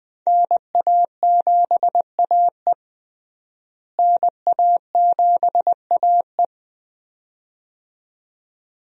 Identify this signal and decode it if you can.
Morse code, heard as a keyed tone, about 20 wpm, 700 Hz. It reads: NA7AE NA7AE
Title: USA Call Signs - 5 Characters - Repeated 2x - 20wpm